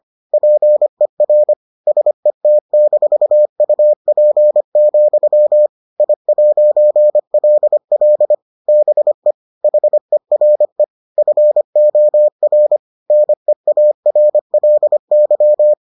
Morse code, heard as a keyed tone, about 25 wpm, 600 Hz. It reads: PER SET-UP, I'LL BE HERE FOR NEARLY